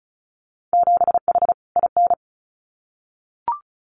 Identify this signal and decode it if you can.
Morse code, heard as a keyed tone, about 36 wpm, 700 Hz. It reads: 7H IN
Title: Winter Field Day Exchanges Encore 35wpm